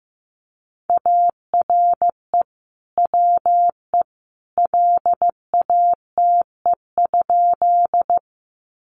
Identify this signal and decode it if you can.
Morse code, heard as a keyed tone, about 15 wpm, 700 Hz. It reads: ARE WE LATE?
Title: Sentences from Top 300 Words 15wpm